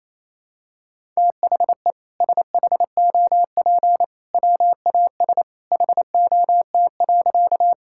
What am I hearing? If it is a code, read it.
Morse code, heard as a keyed tone, about 28 wpm, 700 Hz. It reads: THE SHOP WAS HOT.
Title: Sentences from Top 1600 Words 28wpm